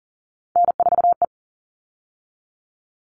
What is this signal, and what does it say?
Morse code, heard as a keyed tone, about 40 wpm, 700 Hz. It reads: N4E
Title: All Cut Numbers - 3-Digit 40wpm